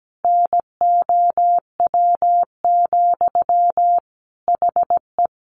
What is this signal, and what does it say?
Morse code, heard as a keyed tone, about 17 wpm, 700 Hz. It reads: NOW, HE
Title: Sentences from Top 700 Words 17wpm